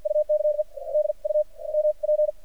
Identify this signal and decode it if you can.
Morse code, heard as a keyed tone, about 52 wpm, spaced as for 37 wpm, 610 Hz. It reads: UKE DUMP
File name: archived / CRSnet